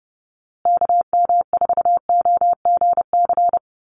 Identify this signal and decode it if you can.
Morse code, heard as a keyed tone, about 30 wpm, 700 Hz. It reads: KM4OGC
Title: USA Call Signs Repeated 3x 30wpm